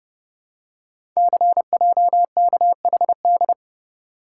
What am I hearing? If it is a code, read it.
Morse code, heard as a keyed tone, about 30 wpm, 700 Hz. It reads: CJKHD
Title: Code Groups - 5 Letter:Numbers - 30wpm